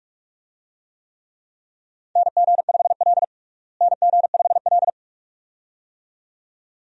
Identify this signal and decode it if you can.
Morse code, heard as a keyed tone, about 44 wpm, 700 Hz. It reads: NG5L NG5L
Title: USA Call Signs - 4 Characters - Repeated 2x - 45wpm